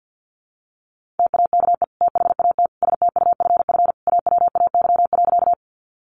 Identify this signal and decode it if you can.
Morse code, heard as a keyed tone, about 50 wpm, 700 Hz. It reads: TAKE THAT STUFF AWAY.
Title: Sentences from Top 1400 Words 50wpm